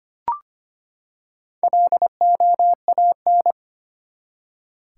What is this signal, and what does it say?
Morse code, heard as a keyed tone, about 25 wpm, 700 Hz. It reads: LOAN
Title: Four Letter Words - 25wpm